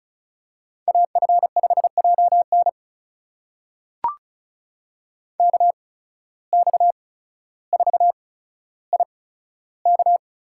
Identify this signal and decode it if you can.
Morse code, heard as a keyed tone, about 36 wpm, spaced as for 15 wpm, 700 Hz. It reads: AF5JN KX4IK
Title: USA Call Signs - 5 Characters - 8x Character Spacing - 35wpm